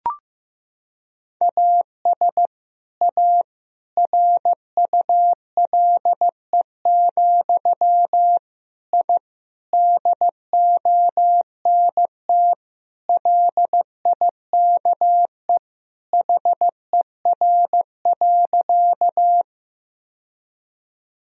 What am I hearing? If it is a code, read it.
Morse code, heard as a keyed tone, about 15 wpm, 700 Hz. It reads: AS A RULE, I DONT LIKE HER.
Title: Sentences from Top 1000 Words - Review Part 2 - 15wpm